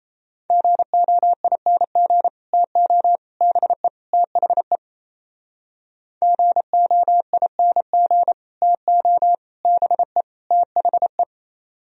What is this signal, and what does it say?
Morse code, heard as a keyed tone, about 36 wpm, 700 Hz. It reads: GOING TO BE THE GOING TO BE THE